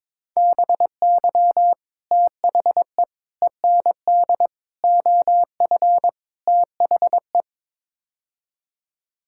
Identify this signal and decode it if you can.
Morse code, heard as a keyed tone, about 22 wpm, 700 Hz. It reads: BY THE END OF THE
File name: practice / 22wpm / Build A Phrase - 5-Word Phrases - 22wpm